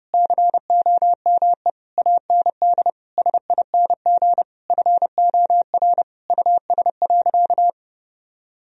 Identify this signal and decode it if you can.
Morse code, heard as a keyed tone, about 30 wpm, 700 Hz. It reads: COME AND SING FOR US.
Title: Sentences from Top 1000 Words - Review Part 2 - 30wpm